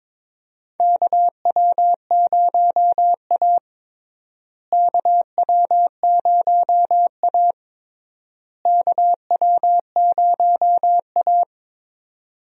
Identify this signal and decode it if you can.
Morse code, heard as a keyed tone, about 22 wpm, 700 Hz. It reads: KW0A KW0A KW0A
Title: USA Call Signs - 4 Characters - Repeated 3x - 22wpm